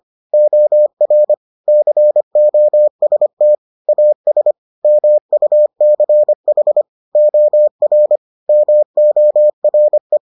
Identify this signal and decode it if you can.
Morse code, heard as a keyed tone, about 25 wpm, 600 Hz. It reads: OR COST AS MUCH OR MORE